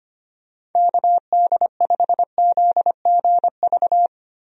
Morse code, heard as a keyed tone, about 25 wpm, 700 Hz. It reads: KD5ZGV
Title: USA Call Signs - 6 Characters - Repeated 2x - 25wpm